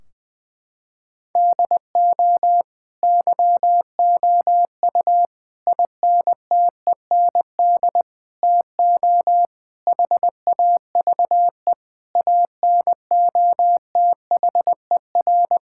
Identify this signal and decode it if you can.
Morse code, heard as a keyed tone, about 21 wpm, 700 Hz. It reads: DO YOU INTEND TO HAVE ANOTHER
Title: Sentences from Top 1600 Words 20wpm